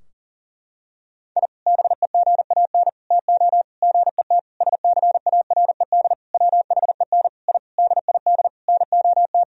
Morse code, heard as a keyed tone, about 40 wpm, 700 Hz. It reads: I BEGAN TO GET SCARED WHEN I DID NOT
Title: Sentences from Top 1000 Words - Review Part 4 - 40wpm